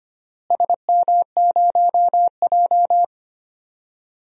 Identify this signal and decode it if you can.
Morse code, heard as a keyed tone, about 25 wpm, 700 Hz. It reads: SM0J
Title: DXCC-100 Call Signs - 4 Characters - Repeated 3x - 25wpm